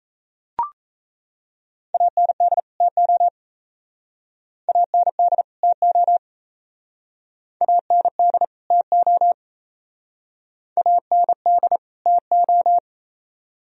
Morse code, heard as a keyed tone, about 34 wpm, 700 Hz. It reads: AND TO AND TO AND TO AND TO
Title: Speed-Racing - Sets of 2 Words 28wpm